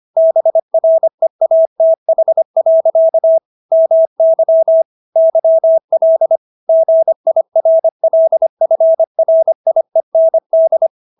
Morse code, heard as a keyed tone, about 25 wpm, 650 Hz. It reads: BREATH. MY YL GIRLFRIEND